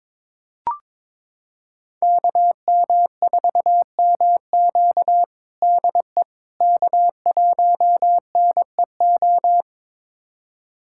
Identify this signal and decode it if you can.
Morse code, heard as a keyed tone, about 22 wpm, 700 Hz. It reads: KM4MQ DE K1NEO
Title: USA Callsign DE Callsign - 5 Characters - 22wpm